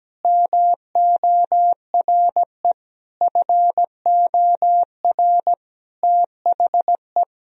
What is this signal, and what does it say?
Morse code, heard as a keyed tone, about 17 wpm, 700 Hz. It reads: MORE FOR THE
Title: Sentences from Top 1000 Words - Review Part 3 - 17wpm